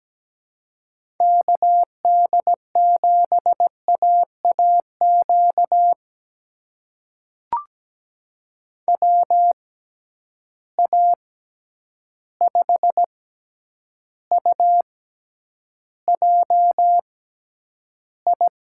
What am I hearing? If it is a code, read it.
Morse code, heard as a keyed tone, about 17 wpm, spaced as for 9 wpm, 700 Hz. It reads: KD7AAQ WA5UJI
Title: USA Call Signs - 6 Characters - 6x Character Spacing - 17wpm